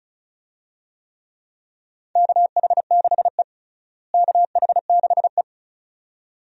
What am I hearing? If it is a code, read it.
Morse code, heard as a keyed tone, about 36 wpm, 700 Hz. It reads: KH6E KH6E
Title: DXCC-100 Call Signs - 4 Characters - Repeated 2x - 35wpm